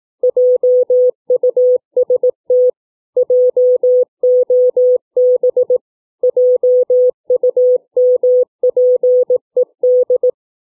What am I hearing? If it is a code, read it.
Morse code, heard as a keyed tone, about 18 wpm, 500 Hz. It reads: JUST JOB JUMPED